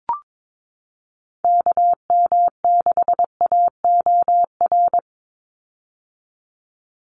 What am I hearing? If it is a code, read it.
Morse code, heard as a keyed tone, about 22 wpm, 700 Hz. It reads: KM6AOR
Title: USA Call Signs 22wpm